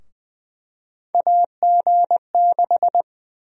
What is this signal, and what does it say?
Morse code, heard as a keyed tone, about 20 wpm, 700 Hz. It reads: AG6
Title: USA Call Sign Prefixes - Repeated 2x - 20wpm